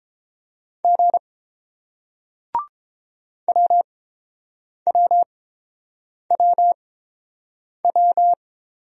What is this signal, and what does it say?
Morse code, heard as a keyed tone, about 30 wpm, spaced as for 7 wpm, 700 Hz. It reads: G WWWW